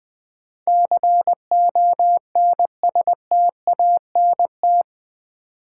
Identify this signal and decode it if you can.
Morse code, heard as a keyed tone, about 20 wpm, 700 Hz. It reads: CONSTANT
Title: Sentences from Top 1500 Words - New Word Drill - 20wpm